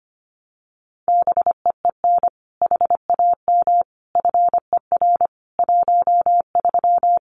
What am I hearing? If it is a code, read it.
Morse code, heard as a keyed tone, about 25 wpm, 700 Hz. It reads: BEEN HAM FER 13
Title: Common QSO Phrases 25wpm